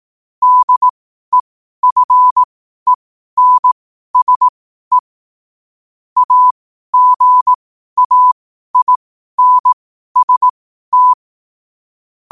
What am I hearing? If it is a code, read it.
Morse code, heard as a keyed tone, about 18 wpm, spaced as for 12 wpm, 1000 Hz. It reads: DEFENSE AGAINST
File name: qotdfiles / 12wpm